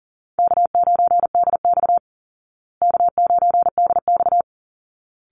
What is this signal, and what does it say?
Morse code, heard as a keyed tone, about 40 wpm, 700 Hz. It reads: K9DX K9DX
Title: Easy Call Signs Repeated 2x - 40wpm